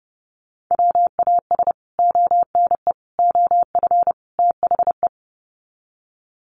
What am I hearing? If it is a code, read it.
Morse code, heard as a keyed tone, about 30 wpm, 700 Hz. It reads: WAS ONE OF THE
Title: Sets of 4 Words 30wpm